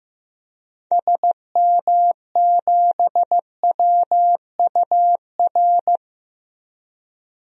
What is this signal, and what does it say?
Morse code, heard as a keyed tone, about 15 wpm, 700 Hz. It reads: SM7WUR
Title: DXCC-100 Call Signs - 6 Characters - 15wpm